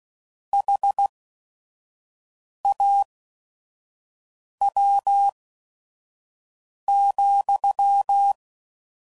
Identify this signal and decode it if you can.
Morse code, heard as a keyed tone, about 16 wpm, spaced as for 5 wpm, 800 Hz. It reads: HAW,